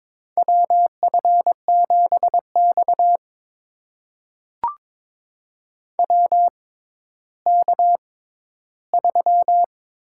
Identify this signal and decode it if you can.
Morse code, heard as a keyed tone, about 23 wpm, spaced as for 13 wpm, 700 Hz. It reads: WF7X WK3
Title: USA Call Signs - 4 Characters - 6x Character Spacing - 22wpm